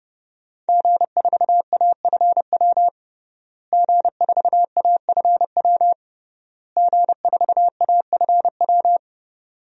Morse code, heard as a keyed tone, about 30 wpm, 700 Hz. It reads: G4AFW G4AFW G4AFW
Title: DXCC-100 Call Signs - 5 Characters - Repeated 3x - 30wpm